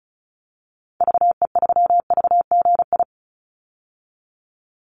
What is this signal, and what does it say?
Morse code, heard as a keyed tone, about 34 wpm, 700 Hz. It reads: VE3VGI
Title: Call Signs Repeated 3x - 35wpm